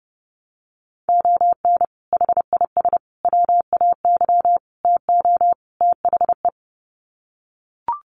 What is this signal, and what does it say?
Morse code, heard as a keyed tone, about 30 wpm, 700 Hz. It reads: ON HIS WAY TO THE
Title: Sets of 5 Words Encore2 30wpm